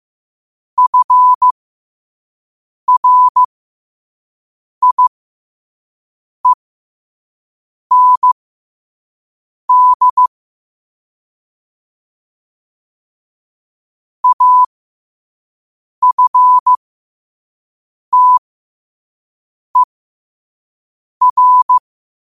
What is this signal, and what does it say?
Morse code, heard as a keyed tone, about 15 wpm, spaced as for 5 wpm, 1000 Hz. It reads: FRIEND AFTER
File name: qotdfiles / 5wpm